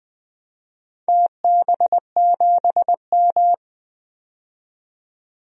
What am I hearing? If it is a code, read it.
Morse code, heard as a keyed tone, about 20 wpm, 700 Hz. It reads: TB7M
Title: DXCC-100 Call Signs - 4 Characters - 20wpm